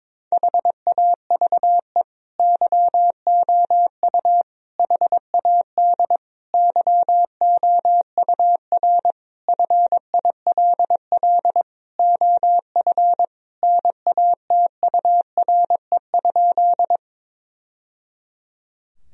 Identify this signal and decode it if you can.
Morse code, heard as a keyed tone, about 22 wpm, 700 Hz. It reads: HAVE YOU HAD YOUR FILL OF NATURE?
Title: Sentences from Top 1000 Words - Review Part 3 - 22wpm